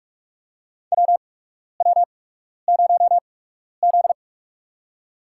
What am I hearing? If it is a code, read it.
Morse code, heard as a keyed tone, about 46 wpm, spaced as for 12 wpm, 700 Hz. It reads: WW0Z
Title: USA Call Signs - 4 Characters - 8x Character Spacing - 45wpm